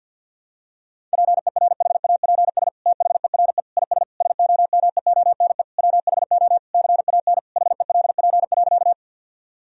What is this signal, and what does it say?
Morse code, heard as a keyed tone, about 50 wpm, 700 Hz. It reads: PERHAPS THERE IS SOMEONE WHO CAN HELP.
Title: Sentences from Top 1000 Words - Review Part 4 - 50wpm